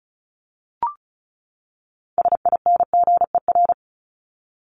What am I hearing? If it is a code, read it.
Morse code, heard as a keyed tone, about 34 wpm, 700 Hz. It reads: SINGER
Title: Top 1501-1750 Words 35wpm